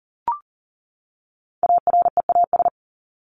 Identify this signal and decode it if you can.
Morse code, heard as a keyed tone, about 39 wpm, 700 Hz. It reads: AREAS
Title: ES:S Suffix - Top 100 Words with Base Words 40wpm